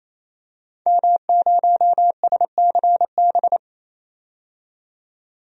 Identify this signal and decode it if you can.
Morse code, heard as a keyed tone, about 28 wpm, 700 Hz. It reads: M0SCB
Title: DXCC-100 Call Signs - 5 Characters - 28wpm